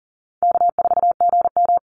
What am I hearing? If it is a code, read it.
Morse code, heard as a keyed tone, about 40 wpm, 700 Hz. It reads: K4GM
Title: Easy Call Signs Repeated 3x - 40wpm